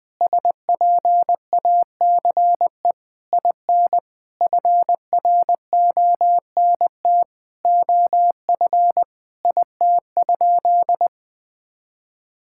Morse code, heard as a keyed tone, about 20 wpm, 700 Hz. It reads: SPACE IN FRONT OF IT?
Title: Sentences from Top 1000 Words - Review Part 5 - 20wpm